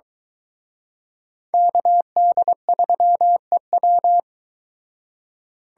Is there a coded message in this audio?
Yes: Morse code from 1.5 to 4.2 s, heard as a keyed tone, about 22 wpm, 700 Hz. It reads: KD3EW